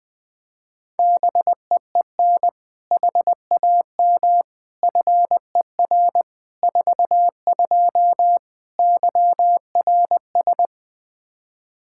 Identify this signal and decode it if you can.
Morse code, heard as a keyed tone, about 20 wpm, 700 Hz. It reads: BEEN HAM FER 42 YRS
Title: Practice QSO - Exchange 3 - More Info - 20wpm